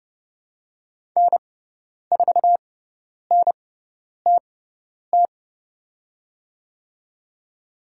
Morse code, heard as a keyed tone, about 30 wpm, spaced as for 10 wpm, 700 Hz. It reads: N4NTT
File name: Lesson 16g - Letter c - Call Signs 30wpm